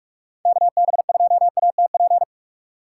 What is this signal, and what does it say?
Morse code, heard as a keyed tone, about 46 wpm, 700 Hz. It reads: KB2ATP
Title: USA Call Signs - 6 Characters - 4x Character Spacing - 45wpm